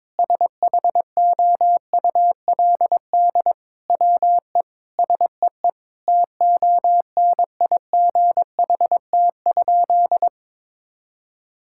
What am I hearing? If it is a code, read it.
Morse code, heard as a keyed tone, about 22 wpm, 700 Hz. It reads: SHOULD WE SEE TONIGHT?
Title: Sentences from Top 1000 Words - Review Part 5 - 22wpm